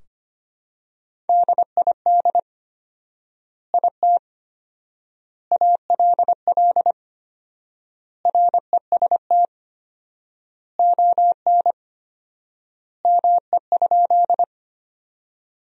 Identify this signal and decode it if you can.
Morse code, heard as a keyed tone, about 25 wpm, 700 Hz. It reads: DID IT ALL REST ON ME?
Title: 4x Word-Spacing - Sentences from Top 500 Words - 25wpm